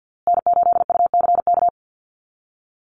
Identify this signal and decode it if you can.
Morse code, heard as a keyed tone, about 50 wpm, 700 Hz. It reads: N8UCK
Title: USA Call Signs Repeated 2x 50wpm